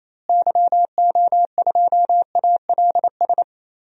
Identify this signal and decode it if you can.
Morse code, heard as a keyed tone, about 28 wpm, 700 Hz. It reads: YO2ALS
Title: DXCC-100 Call Signs - 6 Characters - Repeated 3x - 28wpm